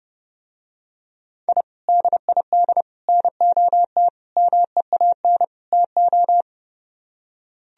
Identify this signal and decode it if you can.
Morse code, heard as a keyed tone, about 30 wpm, 700 Hz. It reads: I DID NOT MEAN TO